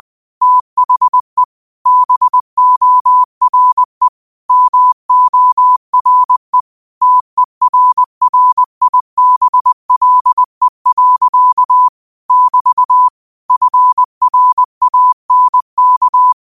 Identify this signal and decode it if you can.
Morse code, heard as a keyed tone, about 20 wpm, 1000 Hz. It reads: THE BORE MORE TERRIBLE. <BT> FRANK
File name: qotdfiles / 20wpm